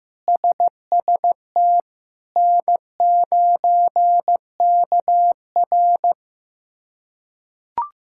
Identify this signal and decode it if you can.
Morse code, heard as a keyed tone, about 15 wpm, 700 Hz. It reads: SST N9KR
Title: K1USN SST QSO Practice - 15wpm